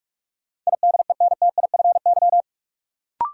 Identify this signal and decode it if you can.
Morse code, heard as a keyed tone, about 47 wpm, 700 Hz. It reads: IDENTIFY
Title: Sentences from Top 900 Words - New Word Drill - 45wpm